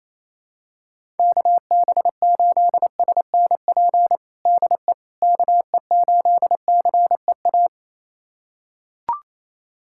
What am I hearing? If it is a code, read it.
Morse code, heard as a keyed tone, about 28 wpm, 700 Hz. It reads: KB8SNP DE KE8CEA
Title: USA Callsign DE Callsign - 6 Characters - 28wpm